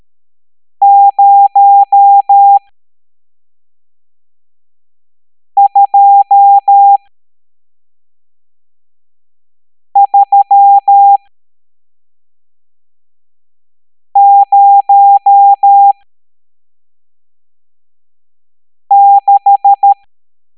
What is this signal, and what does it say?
Morse code, heard as a keyed tone, about 13 wpm, spaced as for 3 wpm, 800 Hz. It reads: 02306
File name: hamdata / L12-EX02-CHIFFRES